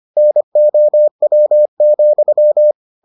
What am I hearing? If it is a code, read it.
Morse code, heard as a keyed tone, about 25 wpm, 600 Hz. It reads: NOW,